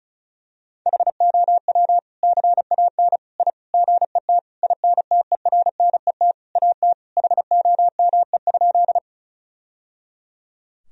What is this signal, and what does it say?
Morse code, heard as a keyed tone, about 34 wpm, 700 Hz. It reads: HOW CAN I GET INTERNET AT HOME?
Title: Sentences from Top 1000 Words - Review Part 4 - 35wpm